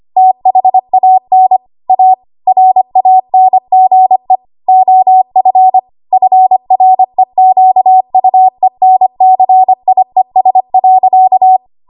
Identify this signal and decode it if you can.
Morse code, heard as a keyed tone, about 25 wpm, 750 Hz. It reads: THAN A RANGE OF FREQUENCIES.
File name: Archive / 25 WPM / 25WPM